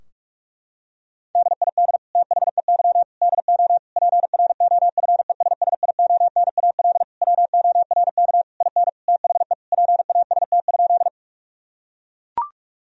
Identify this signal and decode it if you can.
Morse code, heard as a keyed tone, about 42 wpm, 700 Hz. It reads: DID THEY DO PROFESSIONAL WORK IN THE PAST?
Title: Sentences from Top 1000 Words - Review Part 4 - 45wpm